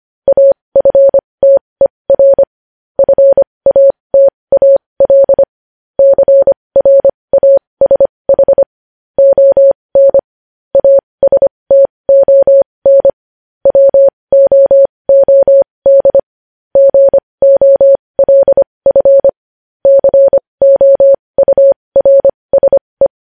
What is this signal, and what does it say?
Morse code, heard as a keyed tone, about 25 wpm, spaced as for 20 wpm, 550 Hz. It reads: AFTER FATAL CRASH ON ASTON WOOD GOLF COURSE
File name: podcast / News headlines-1x-25x20